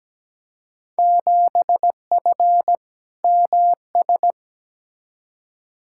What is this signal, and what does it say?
Morse code, heard as a keyed tone, about 17 wpm, 700 Hz. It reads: 7F MS
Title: Field Day Exchanges Encore - 17wpm